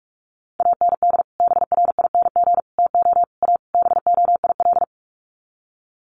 Morse code, heard as a keyed tone, about 44 wpm, 700 Hz. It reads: AND BRING TO A BOIL
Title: Sets of 5 Words Encore 45wpm